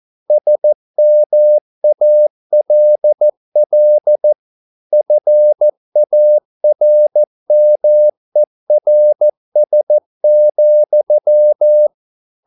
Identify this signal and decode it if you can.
Morse code, heard as a keyed tone, about 14 wpm, 600 Hz. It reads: SMALL FARMERS,